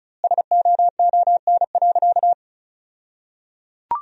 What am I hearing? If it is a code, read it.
Morse code, heard as a keyed tone, about 36 wpm, 700 Hz. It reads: SOON.
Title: 2x Word-Spacing - Sentences from Top 500 Words - 35wpm